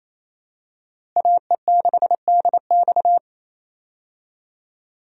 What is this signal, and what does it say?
Morse code, heard as a keyed tone, about 28 wpm, 700 Hz. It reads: AE6DX
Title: USA Call Signs - 5 Characters - Repeated 3x - 28wpm